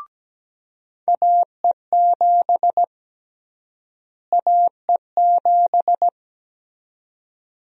Morse code, heard as a keyed tone, about 17 wpm, 700 Hz. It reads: AE7 AE7
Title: USA Call Sign Prefixes - Repeated 2x - 17wpm